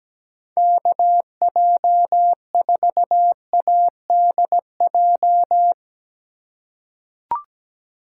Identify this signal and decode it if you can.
Morse code, heard as a keyed tone, about 17 wpm, 700 Hz. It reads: KJ4ADJ
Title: USA Call Signs - 6 Characters - 3x Character Spacing - 17wpm